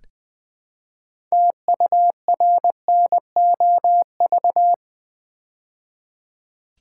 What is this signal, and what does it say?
Morse code, heard as a keyed tone, about 20 wpm, 700 Hz. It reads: TURNOV
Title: Build A Word - 2 Letters At A Time - 8-Letter Words - 20wpm